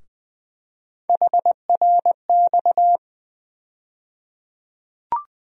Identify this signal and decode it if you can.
Morse code, heard as a keyed tone, about 21 wpm, 700 Hz. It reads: HRX
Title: USA Call Sign Postfixes - 3 Letters 20wpm